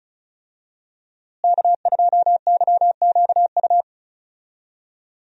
Morse code, heard as a keyed tone, about 36 wpm, 700 Hz. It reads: K2YQU
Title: USA Call Signs - 5 Characters - Repeated 3x - 35wpm